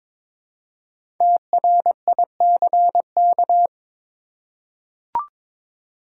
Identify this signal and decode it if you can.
Morse code, heard as a keyed tone, about 23 wpm, 700 Hz. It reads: TRICK
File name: 5 Letter Words - First 3 Letters Given - 22wpm